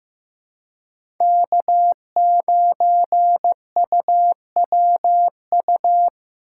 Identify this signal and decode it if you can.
Morse code, heard as a keyed tone, about 15 wpm, 700 Hz. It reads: K9UWU
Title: USA Call Signs - 5 Characters - 4x Character Spacing - 15wpm